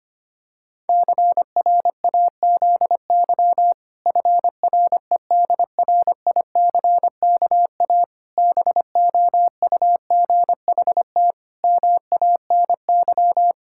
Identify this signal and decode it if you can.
Morse code, heard as a keyed tone, about 25 wpm, 700 Hz. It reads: CRAZY FREDRICKA BOUGHT MANY
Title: Pangram Challenge v2 - 25wpm